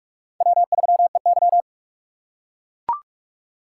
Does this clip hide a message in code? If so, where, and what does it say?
Yes: Morse code from 0.4 to 1.6 s, heard as a keyed tone, about 39 wpm, 700 Hz. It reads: W3EY